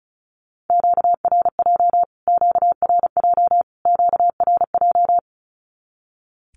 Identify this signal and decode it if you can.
Morse code, heard as a keyed tone, about 34 wpm, 700 Hz. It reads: QRJ QRJ QRJ
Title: Q-Codes Repeated 3x 35wpm